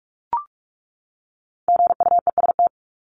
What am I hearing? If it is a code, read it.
Morse code, heard as a keyed tone, about 44 wpm, 700 Hz. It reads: GUEST
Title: Top 1501-1750 Words 45wpm